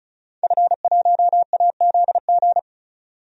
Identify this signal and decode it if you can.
Morse code, heard as a keyed tone, about 36 wpm, 700 Hz. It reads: F1AZG
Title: DXCC-100 Call Signs - 5 Characters - 35wpm